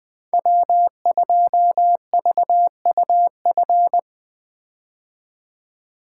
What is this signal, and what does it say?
Morse code, heard as a keyed tone, about 21 wpm, 700 Hz. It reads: W2VUF
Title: K1USN SST QSO Practice - 20wpm